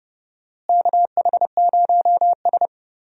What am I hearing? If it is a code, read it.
Morse code, heard as a keyed tone, about 31 wpm, 700 Hz. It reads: KH0S
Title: USA Call Signs - 4 Characters - Repeated 3x - 30wpm